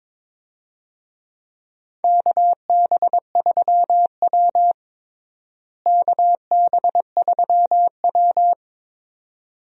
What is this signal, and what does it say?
Morse code, heard as a keyed tone, about 22 wpm, 700 Hz. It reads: KB3W KB3W
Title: USA Call Signs - 4 Characters - Repeated 2x - 22wpm